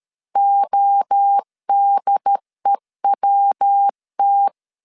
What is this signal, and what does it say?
Morse code, heard as a keyed tone, about 13 wpm, 800 Hz. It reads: ODEWT
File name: CW / DET / 1 3 1500C S C 12wpm 2750